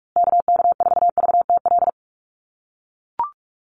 Morse code, heard as a keyed tone, about 44 wpm, 690 Hz. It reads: KK4VTL
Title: USA Call Signs Repeated 2x 45wpm